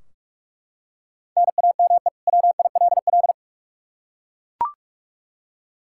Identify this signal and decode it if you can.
Morse code, heard as a keyed tone, about 41 wpm, 700 Hz. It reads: NAME WILL
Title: K1USN SST QSO Practice 45wpm